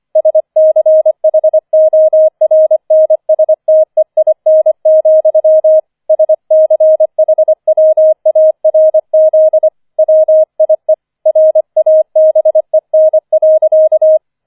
Morse code, heard as a keyed tone, about 25 wpm, 630 Hz. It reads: SCHORNSTEIN, SCHWARZ WIE RABEN.